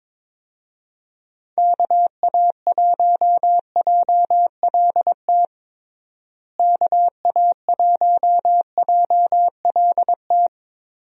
Morse code, heard as a keyed tone, about 22 wpm, 700 Hz. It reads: KA1JLT KA1JLT
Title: USA Call Signs - 6 Characters - Repeated 2x - 22wpm